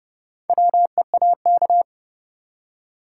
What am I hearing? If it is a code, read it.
Morse code, heard as a keyed tone, about 31 wpm, 700 Hz. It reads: WEAK
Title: Four Letter Words - 30wpm